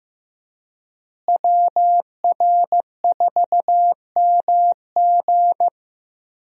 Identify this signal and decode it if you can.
Morse code, heard as a keyed tone, about 15 wpm, 700 Hz. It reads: WR4MG
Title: USA Call Signs - 5 Characters - Repeated 3x - 15wpm